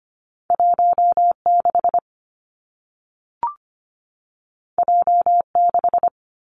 Morse code, heard as a keyed tone, about 25 wpm, 700 Hz. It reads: 1 6 J 6
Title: Two Challenging Letter-Number Combinations 25wpm